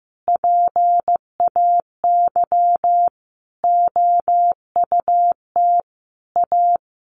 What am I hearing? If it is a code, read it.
Morse code, heard as a keyed tone, about 15 wpm, 700 Hz. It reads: PAY OUT A
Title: Sentences from Top 1500 Words 15wpm